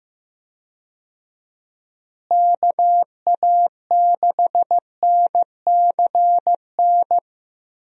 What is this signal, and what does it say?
Morse code, heard as a keyed tone, about 15 wpm, 700 Hz. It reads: KA6NCN
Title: USA Call Signs - 6 Characters - Repeated 2x - 15wpm